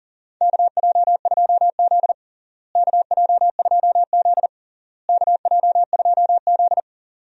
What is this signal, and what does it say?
Morse code, heard as a keyed tone, about 40 wpm, 700 Hz. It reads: KJ2Z KJ2Z KJ2Z
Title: USA Call Signs - 4 Characters - Repeated 3x - 40wpm